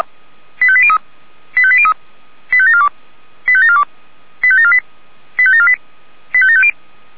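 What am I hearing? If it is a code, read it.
Five-tone selective call (ZVEI), heard as quick runs of five tones, tones of about 70 ms. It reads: 85803 85803 85742 85742 85748 85748 85750
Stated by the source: ZVEI five-tone selective call received on SDR.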